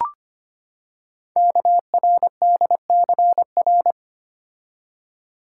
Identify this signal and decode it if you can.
Morse code, heard as a keyed tone, about 25 wpm, 700 Hz. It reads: KRDCR
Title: Code Groups - 5 Letters - 25wpm